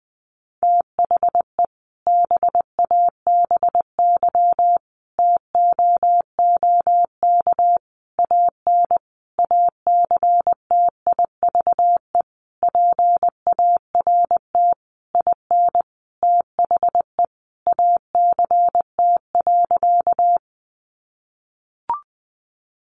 Morse code, heard as a keyed tone, about 20 wpm, 700 Hz. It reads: THE BABY TOOK AN ACTIVE PART IN THE ACT.
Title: Sentences from Top 1300 Words 20wpm